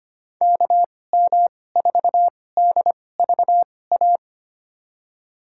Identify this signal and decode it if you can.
Morse code, heard as a keyed tone, about 25 wpm, spaced as for 18 wpm, 690 Hz. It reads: KM4DVA
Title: USA Call Signs - 6 Characters - 2x Character Spacing - 25wpm